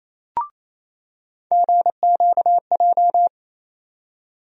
Morse code, heard as a keyed tone, about 28 wpm, 700 Hz. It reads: GQJ
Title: USA Call Sign Postfixes - 3 Letters 28wpm